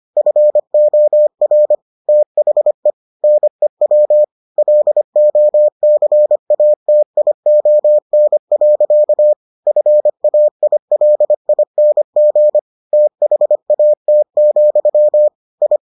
Morse code, heard as a keyed tone, about 25 wpm, 600 Hz. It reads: FOR THE NEW LOCATION. FAILING THAT, I